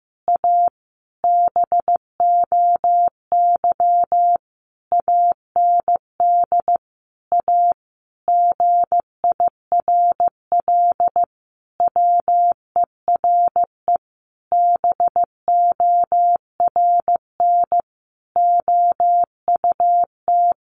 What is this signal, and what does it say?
Morse code, heard as a keyed tone, about 15 wpm, 700 Hz. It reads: A BOY AND A GIRL WERE BORN OUT
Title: Sentences from Top 1200 Words 15wpm